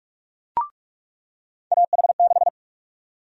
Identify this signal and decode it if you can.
Morse code, heard as a keyed tone, about 45 wpm, 700 Hz. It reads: AH6
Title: USA Call Sign Prefixes - 45wpm